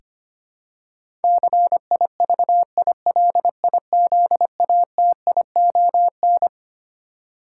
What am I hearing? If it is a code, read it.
Morse code, heard as a keyed tone, about 25 wpm, 700 Hz. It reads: CIVILIZATION
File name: Twelve Letter Words - 25wpm